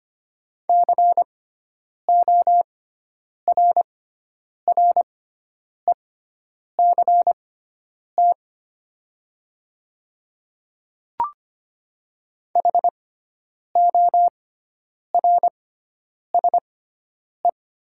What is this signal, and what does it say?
Morse code, heard as a keyed tone, about 25 wpm, spaced as for 9 wpm, 700 Hz. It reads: CORRECT HORSE
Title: 6x Character Spacing - Top 500 Words - 25wpm